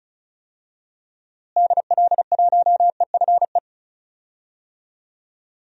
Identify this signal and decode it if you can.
Morse code, heard as a keyed tone, about 36 wpm, 700 Hz. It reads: DL1EFE
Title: DXCC-100 Call Signs - 6 Characters - Repeated 2x - 35wpm